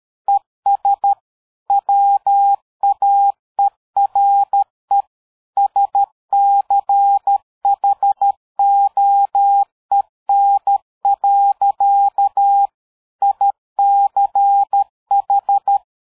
Morse code, heard as a keyed tone, about 13 wpm, 800 Hz. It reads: ES WAERE SCHOEN. ICH
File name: dk5ke / PersWorte 60